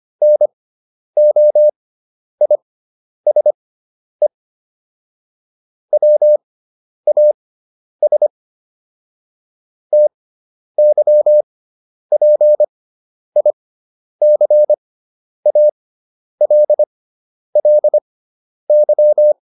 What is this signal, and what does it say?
Morse code, heard as a keyed tone, about 25 wpm, spaced as for 10 wpm, 600 Hz. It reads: NOISE WAS TYPICALLY